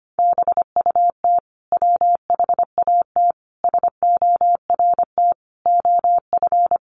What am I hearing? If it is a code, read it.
Morse code, heard as a keyed tone, about 25 wpm, 700 Hz. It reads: BUT WHAT SORT OF
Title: Sentences from Top 1300 Words 25wpm